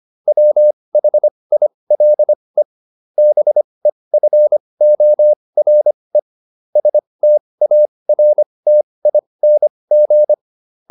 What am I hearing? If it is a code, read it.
Morse code, heard as a keyed tone, about 25 wpm, spaced as for 20 wpm, 600 Hz. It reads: WHILE BEFORE STARTING
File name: SS209 20